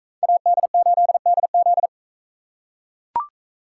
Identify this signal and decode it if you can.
Morse code, heard as a keyed tone, about 42 wpm, 700 Hz. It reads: AD8DZ